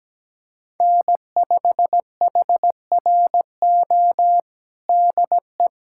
Morse code, heard as a keyed tone, about 17 wpm, 700 Hz. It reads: N5HRO DE
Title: USA Callsign DE Callsign 17wpm